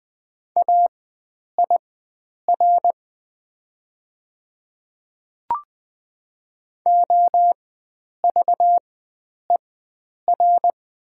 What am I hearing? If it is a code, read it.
Morse code, heard as a keyed tone, about 20 wpm, spaced as for 9 wpm, 700 Hz. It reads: AIR OVER
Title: 4x Character Spacing - Top 300 Words - 20wpm